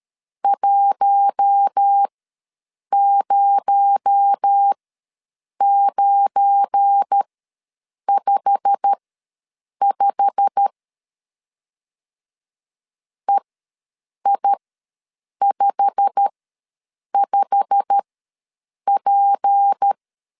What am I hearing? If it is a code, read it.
Morse code, heard as a keyed tone, about 13 wpm, spaced as for 7 wpm, 800 Hz. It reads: 10955 EI55P
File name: LEZIONI / 8 YJ45 / 8 YJ45 ws 12wpm 3146